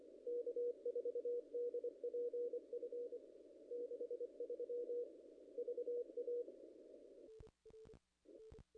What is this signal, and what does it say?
Morse code, heard as a keyed tone, about 23 wpm, 470 Hz. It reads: K4DPF 63 VA
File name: contests / ss-cw-k5tr-op-n5rz -0000